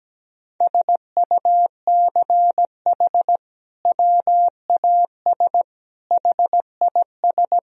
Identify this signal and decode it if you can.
Morse code, heard as a keyed tone, about 17 wpm, 700 Hz. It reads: SUCH WAS HIS
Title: Sentences from Top 1000 Words - Review Part 1 - 17wpm